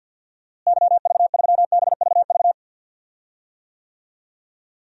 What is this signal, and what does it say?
Morse code, heard as a keyed tone, about 52 wpm, 700 Hz. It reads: YV3BVV
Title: DXCC-100 Call Signs - 6 Characters - Repeated 3x - 50wpm